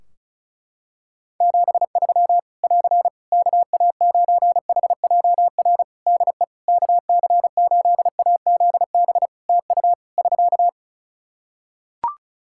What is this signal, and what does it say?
Morse code, heard as a keyed tone, about 36 wpm, 700 Hz. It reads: 73 <AR> KA9HJR DE KC8AZB TU <SK>
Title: Practice QSO - Exchange 4 - Ending - 35wpm